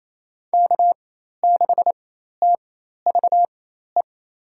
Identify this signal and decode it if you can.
Morse code, heard as a keyed tone, about 28 wpm, spaced as for 13 wpm, 700 Hz. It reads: K6TVE
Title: USA Call Signs - 4x Character Spacing - 28wpm